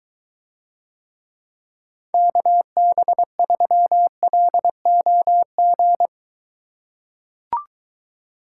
Morse code, heard as a keyed tone, about 23 wpm, 700 Hz. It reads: KB3LOG